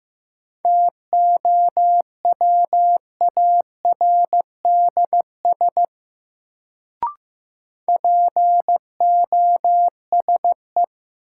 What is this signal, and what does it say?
Morse code, heard as a keyed tone, about 15 wpm, 700 Hz. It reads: TOWARDS POSE
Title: Sentences from Top 1000 Words - Review Part 2 - New Word Drill - 15wpm